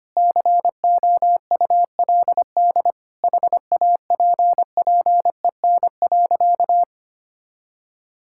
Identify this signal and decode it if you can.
Morse code, heard as a keyed tone, about 25 wpm, 700 Hz. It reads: COULD HAPPEN.
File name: practice / 25wpm / Sentences with 9-Word Phrases - 25wpm